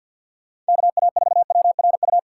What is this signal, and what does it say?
Morse code, heard as a keyed tone, about 54 wpm, 700 Hz. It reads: KA4WRU
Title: USA Call Signs - 6 Characters - 3x Character Spacing - 50wpm